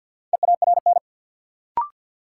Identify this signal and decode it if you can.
Morse code, heard as a keyed tone, about 53 wpm, 700 Hz. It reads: EARN
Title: Sentences from Top 1500 Words - New Word Drill Alt - 50wpm